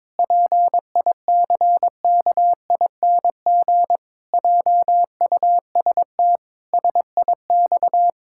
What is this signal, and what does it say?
Morse code, heard as a keyed tone, about 22 wpm, 700 Hz. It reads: PICKING JUST SIX
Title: Pangram Challenge v2 - 22wpm